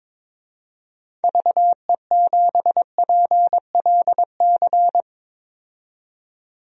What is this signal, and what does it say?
Morse code, heard as a keyed tone, about 22 wpm, 700 Hz. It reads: VE7PLC
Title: DXCC-100 Call Signs - 6 Characters - 22wpm